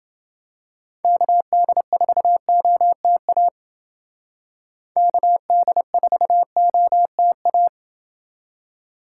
Morse code, heard as a keyed tone, about 28 wpm, 700 Hz. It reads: KD4OTA KD4OTA